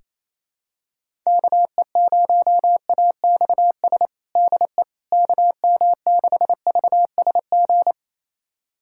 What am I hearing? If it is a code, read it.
Morse code, heard as a keyed tone, about 27 wpm, 700 Hz. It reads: KE0AXS DE KM6VSG
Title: USA Callsign DE Callsign 28wpm